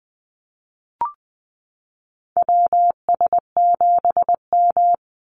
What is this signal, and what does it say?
Morse code, heard as a keyed tone, about 20 wpm, 700 Hz. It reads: WS7M
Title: Call Signs - 20wpm